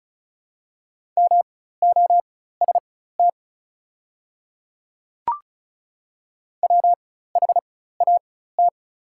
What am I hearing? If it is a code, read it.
Morse code, heard as a keyed tone, about 36 wpm, spaced as for 16 wpm, 700 Hz. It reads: MOST WHAT
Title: 4x Character Spacing - Top 100 Words - 35wpm